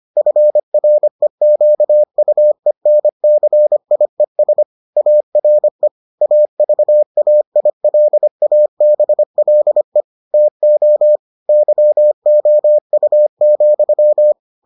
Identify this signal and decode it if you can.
Morse code, heard as a keyed tone, about 25 wpm, 600 Hz. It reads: FREQUENCIES ARE AVAILABLE TO YOU,